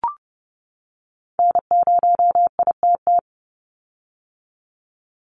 Morse code, heard as a keyed tone, about 30 wpm, 700 Hz. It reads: N0ITT
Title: USA Call Signs 30wpm